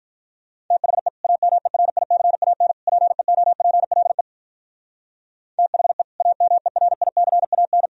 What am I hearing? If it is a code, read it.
Morse code, heard as a keyed tone, about 44 wpm, 700 Hz. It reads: THE AMERICAN PEOPLE THE AMERICAN